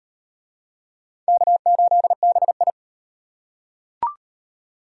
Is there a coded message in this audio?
Yes: Morse code from 1.3 to 2.7 s, heard as a keyed tone, about 38 wpm, 700 Hz. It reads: K8BI